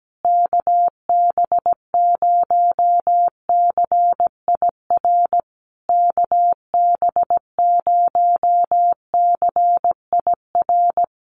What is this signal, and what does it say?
Morse code, heard as a keyed tone, about 17 wpm, 700 Hz. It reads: KB0CIR KB0CIR
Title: USA Call Signs Repeated 3x 17wpm